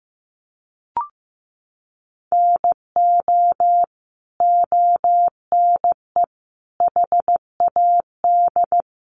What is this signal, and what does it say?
Morse code, heard as a keyed tone, about 15 wpm, 700 Hz. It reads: NO ONE HAD
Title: Sentences from Top 1400 Words 15wpm